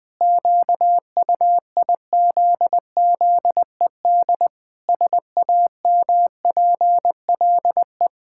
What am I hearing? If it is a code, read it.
Morse code, heard as a keyed tone, about 20 wpm, 700 Hz. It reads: QUIZZED SAMPLE
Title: Pangram Challenge v2 - 20wpm